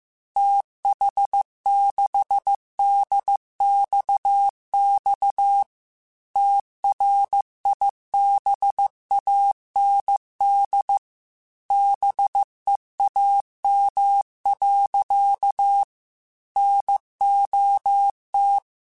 Morse code, heard as a keyed tone, about 15 wpm, 800 Hz. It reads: TH6DXX TRIBAND BEAM. NOT
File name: QSOs / QSOs-mp3s / WD0BGP 1@10